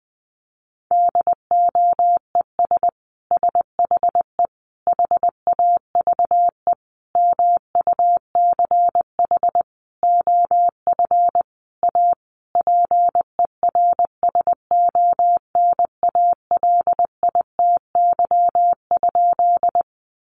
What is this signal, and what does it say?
Morse code, heard as a keyed tone, about 19 wpm, 700 Hz. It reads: DOES SHE HAVE MUCH OF A PERSONALITY?
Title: Sentences from Top 1400 Words 20wpm